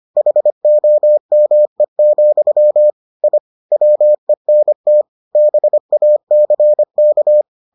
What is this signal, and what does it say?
Morse code, heard as a keyed tone, about 25 wpm, 600 Hz. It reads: HOME, I WENT BACK